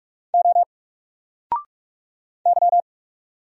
Morse code, heard as a keyed tone, about 44 wpm, 700 Hz. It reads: O Y
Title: Single Character (Letter:Number:Punctuation:Prosign) 45wpm